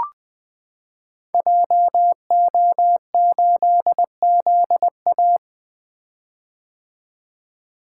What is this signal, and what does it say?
Morse code, heard as a keyed tone, about 20 wpm, 700 Hz. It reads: JO8ZA
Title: Code Groups - 5 Letter:Numbers - 20wpm